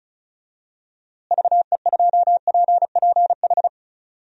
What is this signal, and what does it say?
Morse code, heard as a keyed tone, about 36 wpm, 700 Hz. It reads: VE2PPH
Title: DXCC-100 Call Signs - 6 Characters - Repeated 3x - 35wpm